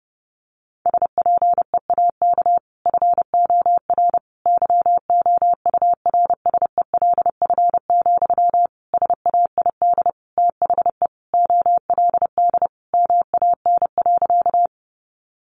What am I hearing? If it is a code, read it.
Morse code, heard as a keyed tone, about 30 wpm, 700 Hz. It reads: SPEAK FOR YOURSELF, SAID THE OLD MAN.
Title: Sentences from Top 1100 Words 30wpm